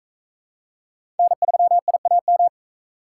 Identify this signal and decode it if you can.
Morse code, heard as a keyed tone, about 40 wpm, 700 Hz. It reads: N3IAM